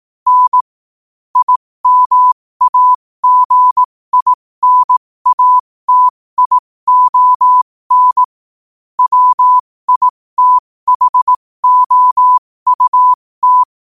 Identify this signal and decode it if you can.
Morse code, heard as a keyed tone, about 18 wpm, spaced as for 15 wpm, 1000 Hz. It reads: N IMAGINATION WITHOUT
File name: qotdfiles / 15wpm